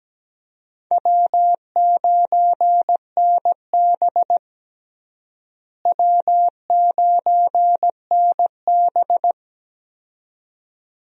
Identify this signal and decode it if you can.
Morse code, heard as a keyed tone, about 17 wpm, 700 Hz. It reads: W9NB W9NB
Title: USA Call Signs - 4 Characters - Repeated 2x - 17wpm